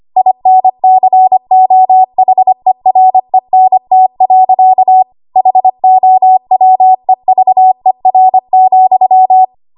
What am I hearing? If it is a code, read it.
Morse code, heard as a keyed tone, about 25 wpm, 750 Hz. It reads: INCOHERENT. HOWEVER,
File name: Archive / 25 WPM / 25WPM